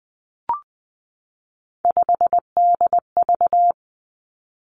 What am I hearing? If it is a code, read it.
Morse code, heard as a keyed tone, about 20 wpm, 700 Hz. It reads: 5DV
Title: Three Challenging Letter-Number Combinations 20wpm